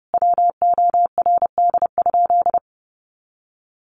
Morse code, heard as a keyed tone, about 30 wpm, 700 Hz. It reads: WORD?
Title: Sentences from Top 1500 Words 30wpm